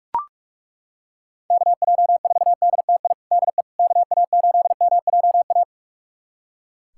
Morse code, heard as a keyed tone, about 42 wpm, 690 Hz. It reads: KJ4DTI DE KA8MJA
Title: USA Callsign DE Callsign 45wpm